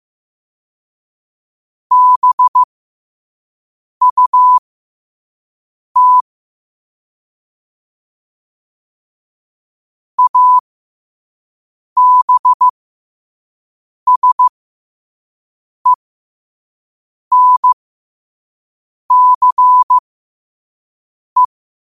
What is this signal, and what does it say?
Morse code, heard as a keyed tone, about 15 wpm, spaced as for 5 wpm, 1000 Hz. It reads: BUT ABSENCE
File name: qotdfiles / 5wpm